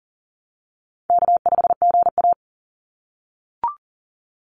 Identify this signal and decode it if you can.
Morse code, heard as a keyed tone, about 41 wpm, 700 Hz. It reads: K5GA
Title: Call Signs - 40wpm